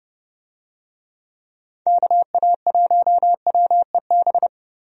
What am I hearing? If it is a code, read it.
Morse code, heard as a keyed tone, about 30 wpm, 700 Hz. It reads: KA1WEB
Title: USA Call Signs - 6 Characters - 30wpm